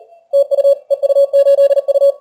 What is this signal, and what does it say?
Morse code, heard as a keyed tone, about 35 wpm, 560 Hz. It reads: TU EU8U
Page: Сегодня решил проверить динамический диапазон приемника в экстримальных условиях теста. Плотность станций на диапазонах большая.